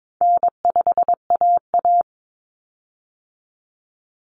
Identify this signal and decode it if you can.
Morse code, heard as a keyed tone, about 22 wpm, 700 Hz. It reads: N5AA
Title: Common Cut Numbers - 4-Digit 22wpm